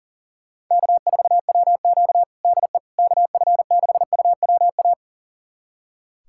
Morse code, heard as a keyed tone, about 37 wpm, 700 Hz. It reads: K4WQ DE KF6UWA
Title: USA Callsign DE Callsign 40wpm